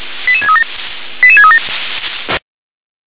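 Five-tone selective call (ZVEI), heard as quick runs of five tones, tones of about 70 ms. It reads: 88637 88637
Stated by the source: ZVEI five-tone call received by SDR.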